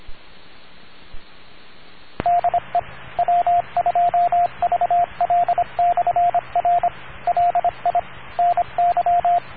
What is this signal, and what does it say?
Morse code, heard as a keyed tone, about 25 wpm, 700 Hz, with radio noise. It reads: DE W2VL/R LI NY